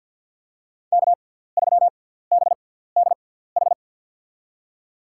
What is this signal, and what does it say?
Morse code, heard as a keyed tone, about 53 wpm, spaced as for 17 wpm, 700 Hz. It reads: K3BDH
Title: USA Call Signs - 5 Characters - 6x Character Spacing - 50wpm